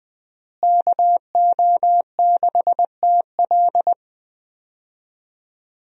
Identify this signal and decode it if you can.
Morse code, heard as a keyed tone, about 21 wpm, 700 Hz. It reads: KO6TL
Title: USA Call Signs - 5 Characters - Repeated 2x - 20wpm